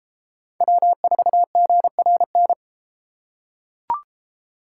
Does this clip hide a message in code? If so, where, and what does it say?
Yes: Morse code from 0.6 to 2.5 s, heard as a keyed tone, about 32 wpm, 700 Hz. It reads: W4GRN